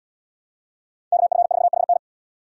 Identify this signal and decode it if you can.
Morse code, heard as a keyed tone, about 77 wpm, 700 Hz. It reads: KF4HN